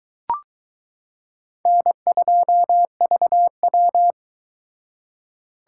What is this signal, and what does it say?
Morse code, heard as a keyed tone, about 22 wpm, 700 Hz. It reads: N2VW